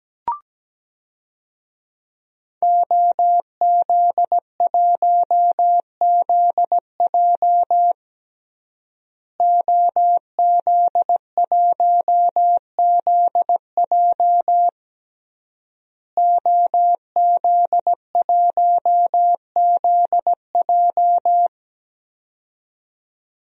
Morse code, heard as a keyed tone, about 17 wpm, 700 Hz. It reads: OZ1ZJ OZ1ZJ OZ1ZJ
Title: DXCC-100 Call Signs - 5 Characters - Repeated 3x - 17wpm